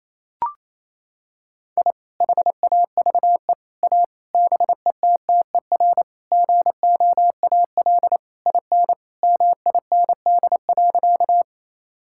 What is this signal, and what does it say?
Morse code, heard as a keyed tone, about 28 wpm, 700 Hz. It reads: I HAVE A BETTER GOAL IN MIND.
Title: Sentences from Top 1000 Words - Review Part 5 - 28wpm